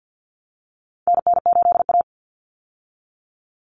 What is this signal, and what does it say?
Morse code, heard as a keyed tone, about 50 wpm, 700 Hz. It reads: NN8A
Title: Common Cut Numbers - 4-Digit 50wpm